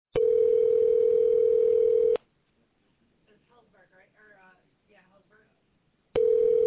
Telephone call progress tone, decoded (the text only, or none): ringback tone